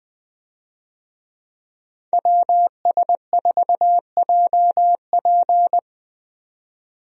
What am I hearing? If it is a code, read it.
Morse code, heard as a keyed tone, about 20 wpm, 700 Hz. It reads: WS4JP
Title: USA Call Signs - 5 Characters - 20wpm